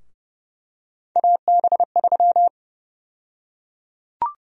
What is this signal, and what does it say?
Morse code, heard as a keyed tone, about 32 wpm, 700 Hz. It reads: AB3
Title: USA Call Sign Prefixes - 30wpm